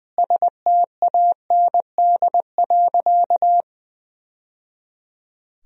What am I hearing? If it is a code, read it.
Morse code, heard as a keyed tone, about 21 wpm, 700 Hz. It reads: STAND.
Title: Sentences from Top 1000 Words - Review Part 1 - 20wpm